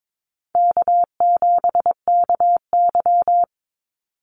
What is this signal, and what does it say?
Morse code, heard as a keyed tone, about 22 wpm, 700 Hz. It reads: K7KY
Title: Easy Call Signs Repeated 2x - 22wpm